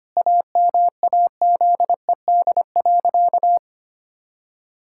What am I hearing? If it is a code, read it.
Morse code, heard as a keyed tone, about 25 wpm, 700 Hz. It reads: AMAZED.
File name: practice / 25wpm / Sentences with 4-Word Phrases - 25wpm